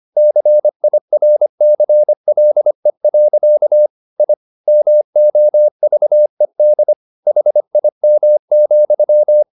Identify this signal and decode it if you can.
Morse code, heard as a keyed tone, about 25 wpm, 600 Hz. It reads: CIRCLE. I MOVED HIM,